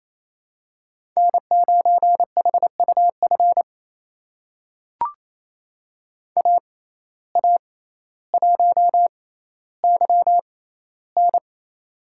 Morse code, heard as a keyed tone, about 28 wpm, spaced as for 14 wpm, 700 Hz. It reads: N9HUF AA1YN
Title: USA Call Signs - 5 Characters - 6x Character Spacing - 28wpm